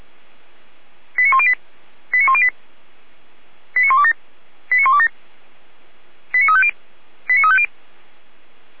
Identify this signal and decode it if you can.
Five-tone selective call (ZVEI), heard as quick runs of five tones, tones of about 70 ms. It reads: 89108 89108 89127 89127 89350 89350